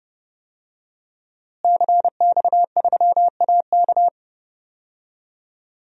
Morse code, heard as a keyed tone, about 32 wpm, 700 Hz. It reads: CX3AK
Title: DXCC-100 Call Signs - 5 Characters - Repeated 2x - 30wpm